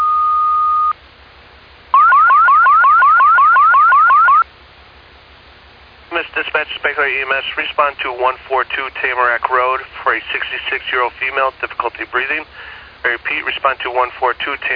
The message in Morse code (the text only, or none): none